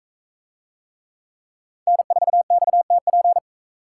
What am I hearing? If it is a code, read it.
Morse code, heard as a keyed tone, about 39 wpm, 700 Hz. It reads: N4XTP